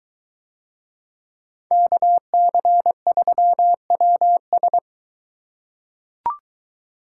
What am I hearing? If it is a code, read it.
Morse code, heard as a keyed tone, about 22 wpm, 700 Hz. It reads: KC3WS